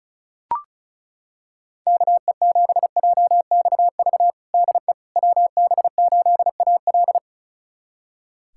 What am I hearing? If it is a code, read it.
Morse code, heard as a keyed tone, about 34 wpm, 700 Hz. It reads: KE7JXV DE WB8AL
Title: USA Callsign DE Callsign 35wpm